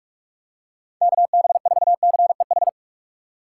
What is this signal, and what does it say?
Morse code, heard as a keyed tone, about 44 wpm, 700 Hz. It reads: KB4CEH
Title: USA Call Signs - 6 Characters - 2x Character Spacing - 45wpm